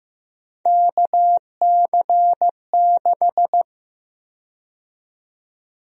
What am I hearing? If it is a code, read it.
Morse code, heard as a keyed tone, about 15 wpm, 700 Hz. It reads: KC6
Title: USA Call Sign Prefixes - 15wpm